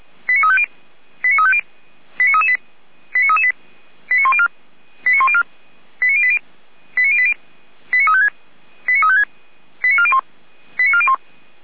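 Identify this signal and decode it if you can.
Five-tone selective call (ZVEI), heard as quick runs of five tones, tones of about 70 ms. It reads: 89350 89350 89338 89338 89104 89104 89080 89080 89356 89356 89401 89401
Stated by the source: Five-tone ZVEI selective call 89164, received on an SDR; from the feuerwehr-orlamuende.org fire brigade audio server.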